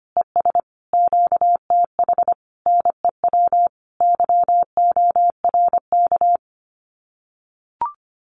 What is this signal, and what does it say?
Morse code, heard as a keyed tone, about 25 wpm, 700 Hz. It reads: ES QTH NEW YORK
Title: Common QSO Phrases Encore 25wpm